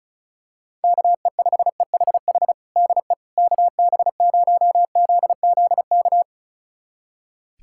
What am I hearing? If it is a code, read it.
Morse code, heard as a keyed tone, about 36 wpm, 700 Hz. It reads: KE5EHH DE KB0ZZK
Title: USA Callsign DE Callsign - 6 Characters - 35wpm